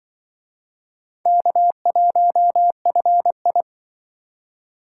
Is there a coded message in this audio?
Yes: Morse code from 1.3 to 3.6 s, heard as a keyed tone, about 25 wpm, 700 Hz. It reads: K1FI